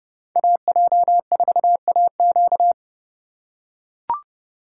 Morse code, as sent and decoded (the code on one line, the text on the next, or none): .- .--- ....- .- --.-
AJ4AQ